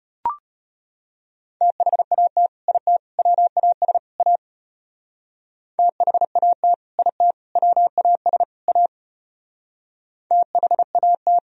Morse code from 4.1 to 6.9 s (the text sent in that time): A THAT